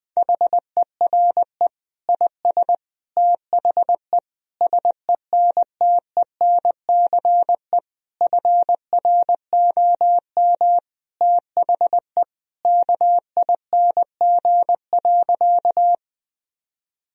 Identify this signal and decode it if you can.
Morse code, heard as a keyed tone, about 21 wpm, 700 Hz. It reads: HERE IS THE SENTENCE FROM THE KING.
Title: Sentences from Top 1000 Words - Review Part 1 - 20wpm